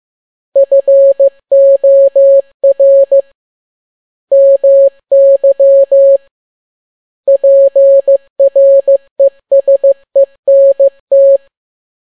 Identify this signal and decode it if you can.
Morse code, heard as a keyed tone, about 15 wpm, 550 Hz. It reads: FOR MY PRESENT